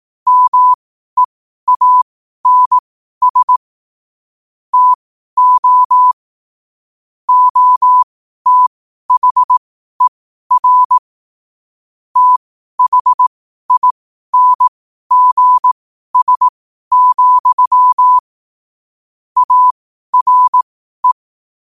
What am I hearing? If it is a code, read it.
Morse code, heard as a keyed tone, about 18 wpm, spaced as for 12 wpm, 1000 Hz. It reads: MEANS TO OTHER THINGS, ARE
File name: qotdfiles / 12wpm